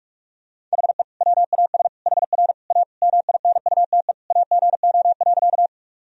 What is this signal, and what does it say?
Morse code, heard as a keyed tone, about 44 wpm, 700 Hz. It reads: HE WAS HR A MINUTE AGO.
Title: Sentences from Top 500 Words and Abbreviations 45wpm